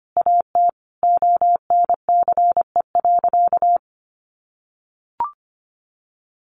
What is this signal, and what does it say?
Morse code, heard as a keyed tone, about 25 wpm, 700 Hz. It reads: AT ONCE.
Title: Sentences from Top 500 Words 25wpm